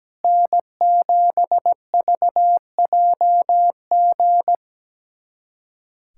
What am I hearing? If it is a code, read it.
Morse code, heard as a keyed tone, about 17 wpm, 700 Hz. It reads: N7VJG
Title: USA Callsign DE Callsign - 5 Characters - 17wpm